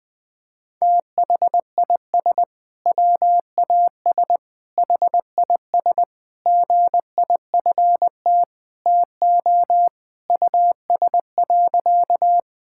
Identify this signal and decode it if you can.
Morse code, heard as a keyed tone, about 20 wpm, 700 Hz. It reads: THIS WAS HIS GIFT TO US.
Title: Sentences from Top 1000 Words - Review Part 6 - 20wpm